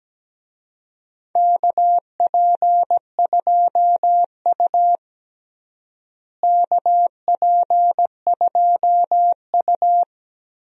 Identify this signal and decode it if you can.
Morse code, heard as a keyed tone, about 17 wpm, 700 Hz. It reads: KP2U KP2U
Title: DXCC-100 Call Signs - 4 Characters - Repeated 2x - 17wpm